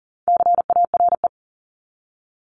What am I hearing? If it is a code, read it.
Morse code, heard as a keyed tone, about 40 wpm, 700 Hz. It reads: CARE
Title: Top 500 Words 40wpm